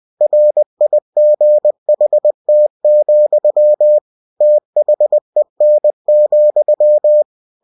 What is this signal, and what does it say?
Morse code, heard as a keyed tone, about 20 wpm, 600 Hz. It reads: RIGHT, THEN,
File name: cw-books / winnie-the-pooh / winnie-20wpm-page-0049